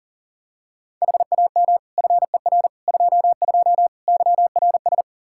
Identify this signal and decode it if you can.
Morse code, heard as a keyed tone, about 40 wpm, 700 Hz. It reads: HAM FER 22 YRS
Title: Practice QSO - Exchange 3 - More Info - 40wpm